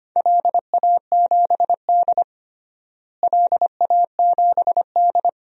Morse code, heard as a keyed tone, about 25 wpm, 700 Hz. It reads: LA7D LA7D
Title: DXCC-100 Call Signs - 4 Characters - Repeated 2x - 25wpm